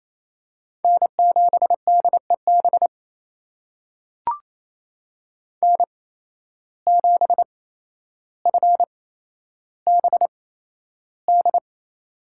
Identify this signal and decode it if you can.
Morse code, heard as a keyed tone, about 28 wpm, spaced as for 12 wpm, 700 Hz. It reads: N7DEB N7FBD
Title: USA Call Signs - 5 Characters - 8x Character Spacing - 28wpm